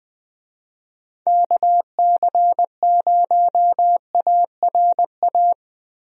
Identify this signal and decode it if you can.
Morse code, heard as a keyed tone, about 20 wpm, 700 Hz. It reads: KC0ARA
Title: USA Call Signs - 6 Characters - Repeated 2x - 20wpm